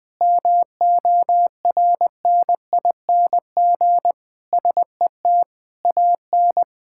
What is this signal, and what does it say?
Morse code, heard as a keyed tone, about 20 wpm, 700 Hz. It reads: MORNING SET AN
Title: Sentences with 8-Word Phrases - Spoken Beginning - 20wpm